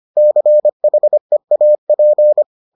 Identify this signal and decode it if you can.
Morse code, heard as a keyed tone, about 25 wpm, 600 Hz. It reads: CHEAP